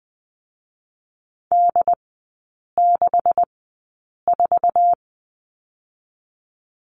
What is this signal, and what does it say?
Morse code, heard as a keyed tone, about 20 wpm, 690 Hz. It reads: D 6 4
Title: 145 Lesson 24e - Letter y - Three Characters 20wpm